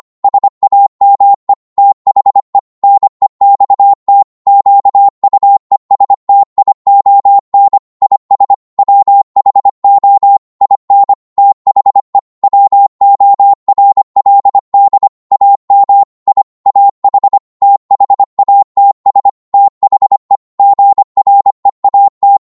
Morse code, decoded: SAME THE NEXT QUESTION IS WHO IN THE WORLD AM I AH THATS THE GREAT